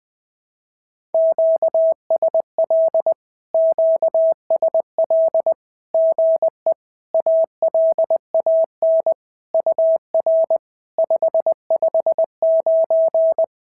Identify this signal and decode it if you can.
Morse code, heard as a keyed tone, about 20 wpm, 650 Hz. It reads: QSL QSL GE ALAN UR 559
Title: SOTA QSO Practice 20wpm